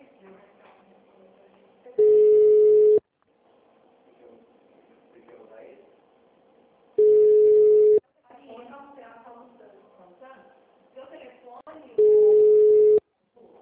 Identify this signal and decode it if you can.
Ringback tone (Europe, 425 Hz), with speech.